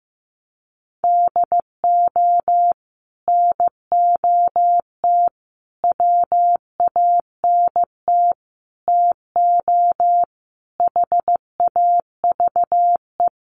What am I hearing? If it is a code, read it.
Morse code, heard as a keyed tone, about 15 wpm, 700 Hz. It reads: DO NOT WANT TO HAVE
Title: Sets of 5 Words 15wpm